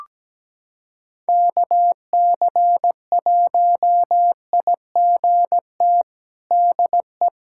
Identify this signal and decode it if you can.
Morse code, heard as a keyed tone, about 17 wpm, 700 Hz. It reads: KC1IGT DE
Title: USA Callsign DE Callsign - 6 Characters - 17wpm